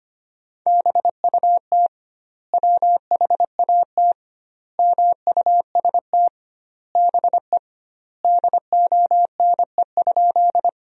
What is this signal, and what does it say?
Morse code, heard as a keyed tone, about 25 wpm, 700 Hz. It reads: BUT WHAT MUST BE DONE?
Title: 2x Word-Spacing - Sentences from Top 500 Words - 25wpm